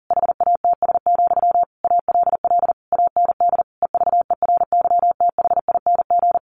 Morse code, heard as a keyed tone, about 40 wpm, 700 Hz. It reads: HATS, ALL AND EVERYTHING
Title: A Sound Of Thunder 40wpm